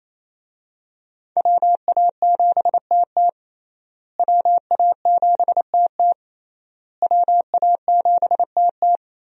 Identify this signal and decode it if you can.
Morse code, heard as a keyed tone, about 28 wpm, 700 Hz. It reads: WA7TT WA7TT WA7TT
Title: USA Call Signs - 5 Characters - Repeated 3x - 28wpm